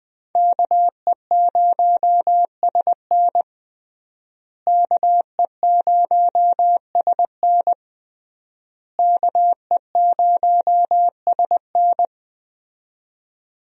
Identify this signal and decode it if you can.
Morse code, heard as a keyed tone, about 21 wpm, 700 Hz. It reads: KE0SN KE0SN KE0SN
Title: USA Call Signs - 5 Characters - Repeated 3x - 20wpm